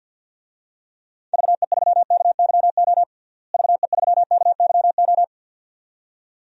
Morse code, heard as a keyed tone, about 52 wpm, 700 Hz. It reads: VE3KYO VE3KYO
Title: DXCC-100 Call Signs - 6 Characters - Repeated 2x - 50wpm